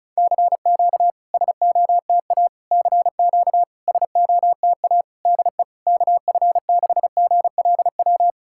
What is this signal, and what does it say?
Morse code, heard as a keyed tone, about 36 wpm, 700 Hz. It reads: CQ SOTA CQ SOTA DE KF6GLW
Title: SOTA QSO Practice 35wpm